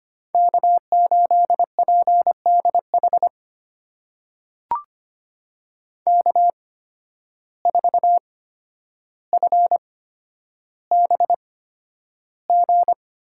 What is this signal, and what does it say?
Morse code, heard as a keyed tone, about 25 wpm, spaced as for 11 wpm, 700 Hz. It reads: K8PDH K4FBG
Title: USA Call Signs - 5 Characters - 8x Character Spacing - 25wpm